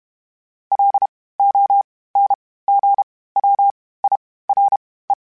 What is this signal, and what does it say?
Morse code, heard as a keyed tone, about 32 wpm, spaced as for 18 wpm, 800 Hz. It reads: LONGWIRE
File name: (spaced) / FromSMs spaced mp3s / Antennas sp@30